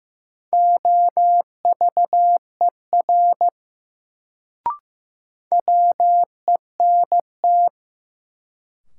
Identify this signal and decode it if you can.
Morse code, heard as a keyed tone, about 15 wpm, 700 Hz. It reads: OVER WENT
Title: Sets of 2 Words - New Word Drill - 15wpm